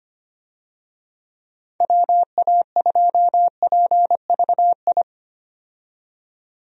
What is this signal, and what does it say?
Morse code, heard as a keyed tone, about 25 wpm, 700 Hz. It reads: WA2PVI
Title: USA Call Signs - 6 Characters - 25wpm